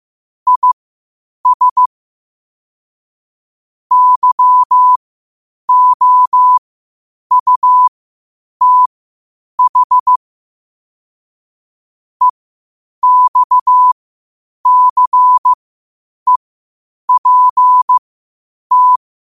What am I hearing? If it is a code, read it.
Morse code, heard as a keyed tone, about 15 wpm, spaced as for 8 wpm, 1000 Hz. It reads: IS YOUTH EXCEPT
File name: qotdfiles / 8wpm